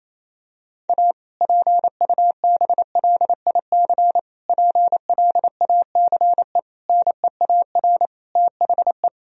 Morse code, heard as a keyed tone, about 28 wpm, 700 Hz. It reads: A PUBLIC PLACE NEAR THE
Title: Sentences from Top 1000 Words - Review Part 3 - 28wpm